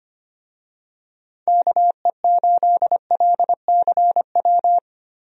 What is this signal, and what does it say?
Morse code, heard as a keyed tone, about 25 wpm, 700 Hz. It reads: KE8LCW
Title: USA Call Signs - 6 Characters - Repeated 2x - 25wpm